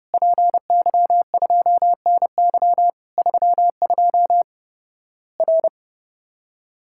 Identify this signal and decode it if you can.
Morse code, heard as a keyed tone, about 32 wpm, 700 Hz. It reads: PY2NY 32 R
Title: Worked All Europe DX Contest - QTC Exchange - 30wpm